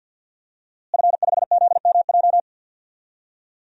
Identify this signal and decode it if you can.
Morse code, heard as a keyed tone, about 53 wpm, 700 Hz. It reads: F5ZMJ
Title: DXCC-100 Call Signs - 5 Characters - Repeated 2x - 50wpm